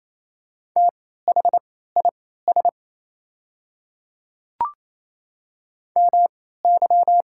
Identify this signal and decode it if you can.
Morse code, heard as a keyed tone, about 28 wpm, spaced as for 16 wpm, 700 Hz. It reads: THIS MY
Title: 3x Character Spacing - Top 200 Words - 28wpm